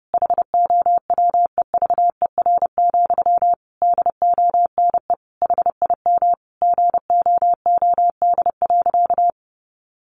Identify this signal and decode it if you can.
Morse code, heard as a keyed tone, about 30 wpm, 700 Hz. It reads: HOWEVER, DONE HIM GOOD.
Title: Sentences from Top 1200 Words 30wpm